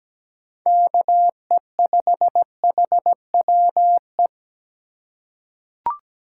Morse code, heard as a keyed tone, about 17 wpm, 700 Hz. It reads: KE5HWE
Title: USA Call Signs - 6 Characters - 4x Character Spacing - 17wpm